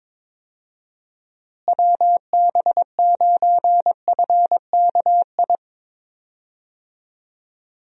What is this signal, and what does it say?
Morse code, heard as a keyed tone, about 22 wpm, 700 Hz. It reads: WB9FKI
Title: USA Call Signs - 6 Characters - Repeated 3x - 22wpm